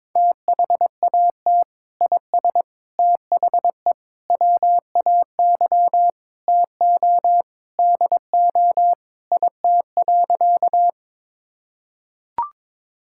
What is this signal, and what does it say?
Morse code, heard as a keyed tone, about 22 wpm, 700 Hz. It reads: THAT IS THE WAY TO DO IT.
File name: practice / 22wpm / Sentences with 3-Word Phrases - 22wpm